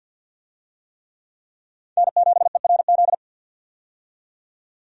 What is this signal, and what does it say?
Morse code, heard as a keyed tone, about 51 wpm, 700 Hz. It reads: N7ERZ
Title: USA Call Signs - 5 Characters - 50wpm